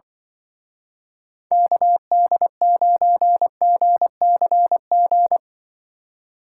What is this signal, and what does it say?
Morse code, heard as a keyed tone, about 25 wpm, 700 Hz. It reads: KD9GCG
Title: Speed Racing - USA 2x3 Call Signs 20wpm